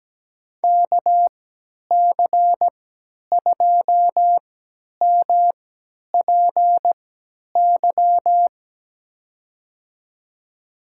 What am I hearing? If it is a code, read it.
Morse code, heard as a keyed tone, about 17 wpm, spaced as for 10 wpm, 700 Hz. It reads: KC2MPY
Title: USA Call Signs - 6 Characters - 3x Character Spacing - 17wpm